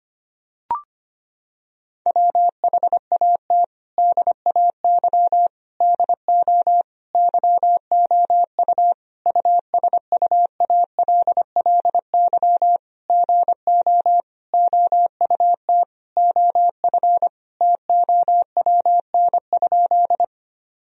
Morse code, heard as a keyed tone, about 25 wpm, 700 Hz. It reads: WHAT DAY DO YOU USUALLY GO OUT OF TOWN?
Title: Sentences from Top 1000 Words - Review Part 3 - 25wpm